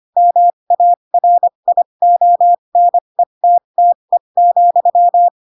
Morse code, heard as a keyed tone, about 25 wpm, spaced as for 22 wpm, 700 Hz. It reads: MARIONETTE,
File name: Pinocchio / Pinocchio ch8 18 25 ik1hge